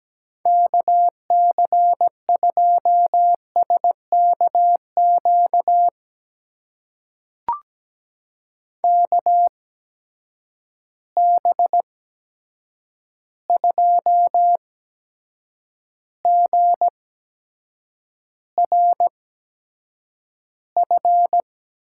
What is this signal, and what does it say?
Morse code, heard as a keyed tone, about 17 wpm, spaced as for 7 wpm, 700 Hz. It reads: KC2SKQ KB2GRF
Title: USA Call Signs - 6 Characters - 8x Character Spacing - 17wpm